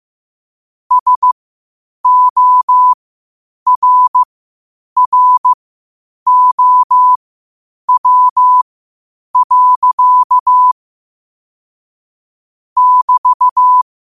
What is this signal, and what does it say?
Morse code, heard as a keyed tone, about 15 wpm, spaced as for 8 wpm, 1000 Hz. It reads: SORROW. <BT>